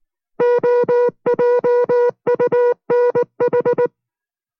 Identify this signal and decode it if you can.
Morse code, heard as a keyed tone, about 19 wpm, 490 Hz. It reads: OJUNH